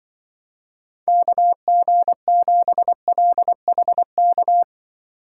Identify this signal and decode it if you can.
Morse code, heard as a keyed tone, about 24 wpm, 700 Hz. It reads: KG7LHK
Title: Speed Racing - USA Call Signs 20wpm